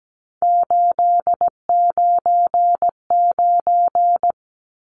Morse code, heard as a keyed tone, about 17 wpm, 700 Hz. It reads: 899
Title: Numbers - 3-Digit 17wpm